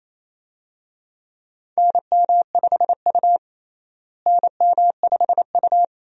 Morse code, heard as a keyed tone, about 28 wpm, 700 Hz. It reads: NM5U NM5U
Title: USA Call Signs - 4 Characters - Repeated 3x - 28wpm